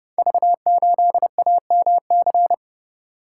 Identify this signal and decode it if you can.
Morse code, heard as a keyed tone, about 30 wpm, 700 Hz. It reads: V8AMC
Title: Code Groups - 5 Letter:Numbers - 30wpm